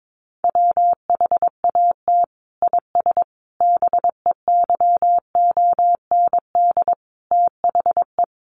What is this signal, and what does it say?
Morse code, heard as a keyed tone, about 22 wpm, 700 Hz. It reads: WHAT IS BEYOND THE
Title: Sentences from Top 1100 Words 22wpm